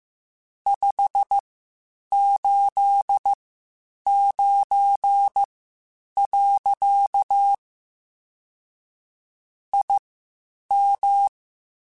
Morse code, heard as a keyed tone, about 15 wpm, spaced as for 8 wpm, 800 Hz. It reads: 589. IM
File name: (spaced) / QSOs Spaced-mp3s / WD0FBT 1 @13 sp